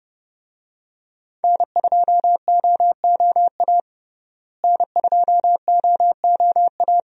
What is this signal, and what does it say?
Morse code, heard as a keyed tone, about 30 wpm, 700 Hz. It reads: N2OOA N2OOA
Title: USA Call Signs - 5 Characters - Repeated 2x - 30wpm